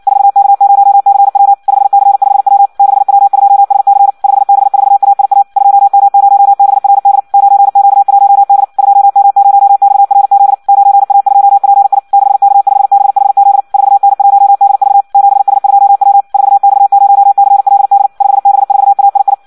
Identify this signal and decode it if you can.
Morse code, heard as a keyed tone, about 56 wpm, 800 Hz. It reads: 4K0CW 5C5W 6W2SC 5B4AIA 2M0BRD 9Q1D 2M0XRZ 9A2PI 6K5BHZ 4N1DV 7S2W 4X0CLD 5B4AII
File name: Audio practica CW / qrq-tx-58 2